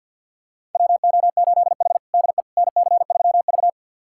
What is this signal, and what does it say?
Morse code, heard as a keyed tone, about 52 wpm, 700 Hz. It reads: WO8H DE NC3V
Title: USA Callsign DE Callsign - 4 Characters - 50wpm